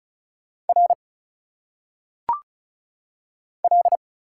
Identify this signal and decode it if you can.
Morse code, heard as a keyed tone, about 36 wpm, 700 Hz. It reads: R L
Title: Single Letters - Phonetic Pronunciation 35wpm